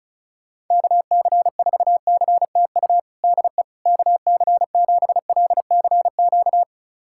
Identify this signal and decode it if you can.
Morse code, heard as a keyed tone, about 36 wpm, 700 Hz. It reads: KC4CTU DE KC7LCQ
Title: USA Callsign DE Callsign - 6 Characters - 35wpm